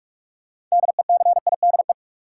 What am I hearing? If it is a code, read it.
Morse code, heard as a keyed tone, about 46 wpm, 700 Hz. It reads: DECIDE
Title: Sentences from Top 500 Words - New Word Drill - 45wpm